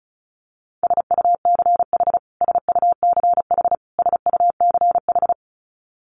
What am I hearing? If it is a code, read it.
Morse code, heard as a keyed tone, about 34 wpm, 700 Hz. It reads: SUCH SUCH SUCH
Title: Top 300 Words Repeated 35wpm